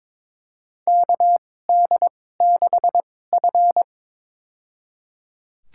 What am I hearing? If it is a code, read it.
Morse code, heard as a keyed tone, about 22 wpm, spaced as for 16 wpm, 700 Hz. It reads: KD6F
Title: USA Call Signs - 4 Characters - 2x Character Spacing - 22wpm